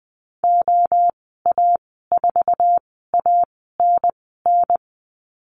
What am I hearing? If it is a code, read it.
Morse code, heard as a keyed tone, about 20 wpm, spaced as for 14 wpm, 700 Hz. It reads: OA4ANN
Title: 069 Lesson 12g - Letter d - Call Signs 20wpm